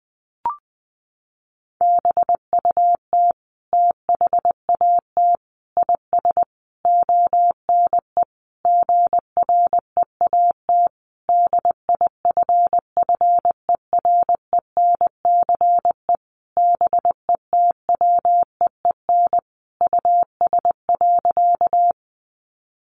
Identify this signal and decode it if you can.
Morse code, heard as a keyed tone, about 20 wpm, 700 Hz. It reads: BUT THAT IS ONE GREAT DIFFERENCE BETWEEN US.
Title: Sentences from Top 800 Words 20wpm